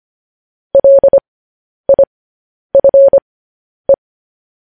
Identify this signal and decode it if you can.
Morse code, heard as a keyed tone, about 25 wpm, spaced as for 10 wpm, 550 Hz. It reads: LIFE
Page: News Headlines in Morse Code at 10 WPM